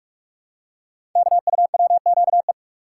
Morse code, heard as a keyed tone, about 44 wpm, 690 Hz. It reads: KUWQE
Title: Code Groups - 5 Letters - 45wpm